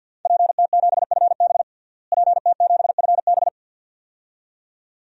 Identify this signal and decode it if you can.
Morse code, heard as a keyed tone, about 52 wpm, 700 Hz. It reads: PT7FB PT7FB
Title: DXCC-100 Call Signs - 5 Characters - Repeated 2x - 50wpm